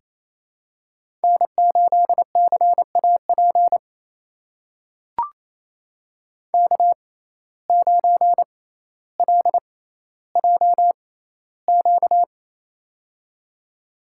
Morse code, heard as a keyed tone, about 28 wpm, spaced as for 14 wpm, 700 Hz. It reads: N8CAP K9LJQ
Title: USA Call Signs - 5 Characters - 6x Character Spacing - 28wpm